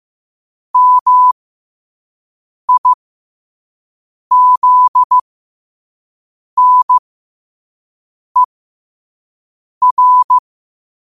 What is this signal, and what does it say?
Morse code, heard as a keyed tone, about 15 wpm, spaced as for 5 wpm, 1000 Hz. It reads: MIZNER
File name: qotdfiles / 5wpm